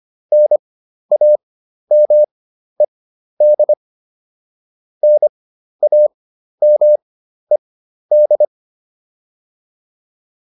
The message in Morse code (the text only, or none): NAMED NAMED